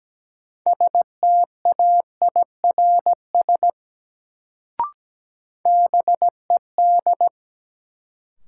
Morse code, heard as a keyed tone, about 17 wpm, 700 Hz. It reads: STAIRS BED
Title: Sets of 5 Words Encore - New Word Drill - 17wpm